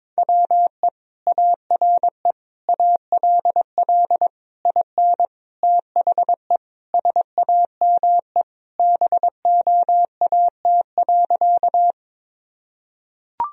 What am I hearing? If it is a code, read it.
Morse code, heard as a keyed tone, about 22 wpm, 700 Hz. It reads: WE ARE ALL IN THE SAME BOAT.
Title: Sentences from Top 1000 Words - Review Part 3 - 22wpm